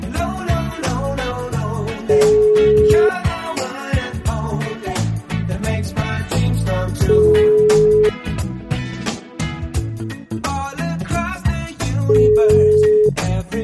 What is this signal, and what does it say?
Ringback tone (Europe, 425 Hz), with noise.